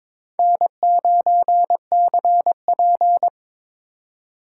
Morse code, heard as a keyed tone, about 26 wpm, 700 Hz. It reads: N9CP
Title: Speed Racing - USA 1x2 Call Signs 22wpm